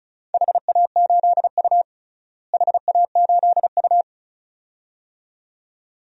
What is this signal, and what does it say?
Morse code, heard as a keyed tone, about 34 wpm, 700 Hz. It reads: HA8U HA8U
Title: DXCC-100 Call Signs - 4 Characters - Repeated 2x - 35wpm